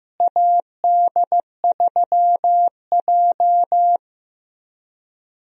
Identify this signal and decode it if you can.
Morse code, heard as a keyed tone, about 16 wpm, 700 Hz. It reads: AD3J
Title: USA Call Signs - 4 Characters - Repeated 3x - 15wpm